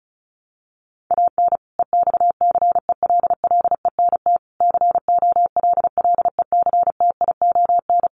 Morse code, heard as a keyed tone, about 34 wpm, 700 Hz. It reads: AN EXCELLENT COLLECTION
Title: Sentences from Top 1200 Words 35wpm